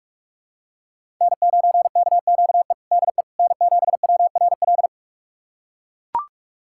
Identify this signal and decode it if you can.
Morse code, heard as a keyed tone, about 41 wpm, 700 Hz. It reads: N9KQE DE N7WRL
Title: USA Callsign DE Callsign - 5 Characters - 45wpm